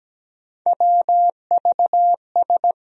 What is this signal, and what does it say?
Morse code, heard as a keyed tone, about 17 wpm, 700 Hz. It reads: WVS
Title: USA Call Sign Postfixes - 3 Letters 17wpm